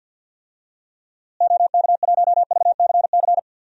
Morse code, heard as a keyed tone, about 47 wpm, 700 Hz. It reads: OK1VCC
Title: DXCC-100 Call Signs - 6 Characters - Repeated 2x - 50wpm